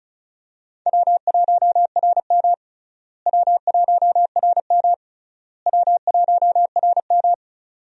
Morse code, heard as a keyed tone, about 36 wpm, 700 Hz. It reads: W1RM W1RM W1RM
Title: USA Call Signs - 4 Characters - Repeated 3x - 35wpm